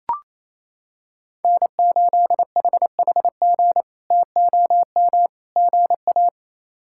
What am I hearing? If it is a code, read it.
Morse code, heard as a keyed tone, about 28 wpm, 700 Hz. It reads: N8HHG TOM GA
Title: K1USN SST QSO Practice - 28wpm